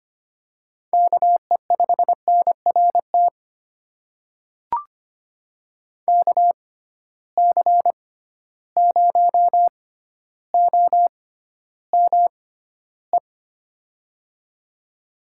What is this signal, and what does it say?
Morse code, heard as a keyed tone, about 25 wpm, spaced as for 13 wpm, 700 Hz. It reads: KE5NRT KC0OME
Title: USA Call Signs - 6 Characters - 6x Character Spacing - 25wpm